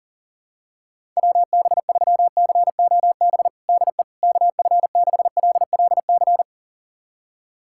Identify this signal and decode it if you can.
Morse code, heard as a keyed tone, about 40 wpm, 700 Hz. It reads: WB3COB DE KF6LLC
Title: USA Callsign DE Callsign - 6 Characters - 40wpm